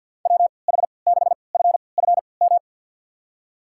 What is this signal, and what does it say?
Morse code, heard as a keyed tone, about 48 wpm, spaced as for 28 wpm, 700 Hz. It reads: WH6FFM
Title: USA Call Signs - 6 Characters - 3x Character Spacing - 50wpm